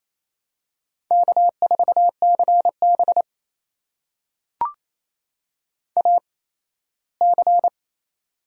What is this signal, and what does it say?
Morse code, heard as a keyed tone, about 28 wpm, spaced as for 17 wpm, 700 Hz. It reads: K4CB AC
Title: USA Call Signs - 8x Character Spacing - 28wpm